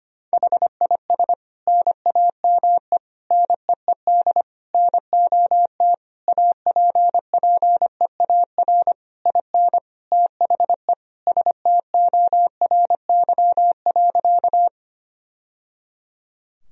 Morse code, heard as a keyed tone, about 25 wpm, 700 Hz. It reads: HIS NAME NEED NOT APPEAR IN THE STORY.
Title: Sentences from Top 1000 Words - Review Part 2 - 25wpm